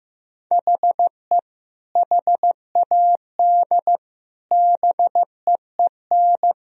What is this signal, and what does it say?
Morse code, heard as a keyed tone, about 15 wpm, 700 Hz. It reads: HE HAD BEEN
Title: Speed-Racing - Sets of 3 Words 15wpm